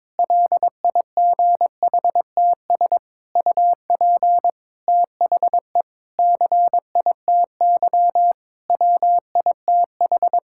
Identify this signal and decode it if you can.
Morse code, heard as a keyed tone, about 22 wpm, 700 Hz. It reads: LIGHTS UP THE CITY WITH
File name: Difficult Sentences with Hints - 22wpm